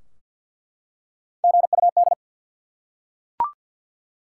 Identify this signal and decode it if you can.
Morse code, heard as a keyed tone, about 50 wpm, 700 Hz. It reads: GUD
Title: Top 180 QSO Elements 50wpm